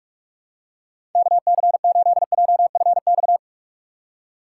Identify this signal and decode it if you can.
Morse code, heard as a keyed tone, about 44 wpm, 700 Hz. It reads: KC8JFX
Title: USA Call Signs - 4x Character Spacing - 45wpm